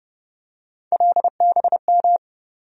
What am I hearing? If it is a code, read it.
Morse code, heard as a keyed tone, about 32 wpm, 700 Hz. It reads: LBM
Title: USA Call Sign Postfixes - 3 Letters 30wpm